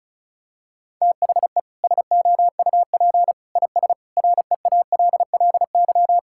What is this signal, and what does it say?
Morse code, heard as a keyed tone, about 34 wpm, 700 Hz. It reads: THE SOUP IS REALLY
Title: Sentences from Top 1000 Words - Review Part 6 - 35wpm